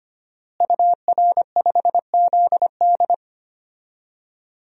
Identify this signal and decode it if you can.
Morse code, heard as a keyed tone, about 25 wpm, 700 Hz. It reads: UR5ZD
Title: DXCC-100 Call Signs - 5 Characters - Repeated 2x - 25wpm